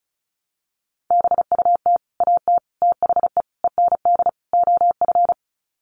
Morse code, heard as a keyed tone, about 34 wpm, 700 Hz. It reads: BUT AT THE END OF
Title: Sets of 5 Words Encore2 35wpm